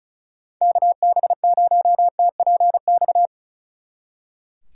Morse code, heard as a keyed tone, about 39 wpm, 700 Hz. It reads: KB0TPX
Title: Speed Racing - USA 2x3 Call Signs 35wpm